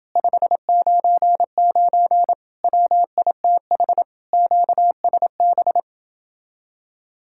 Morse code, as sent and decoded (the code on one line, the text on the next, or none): ..... ----. ----. / .-- .. - .... / --.- ... -...
599 WITH QSB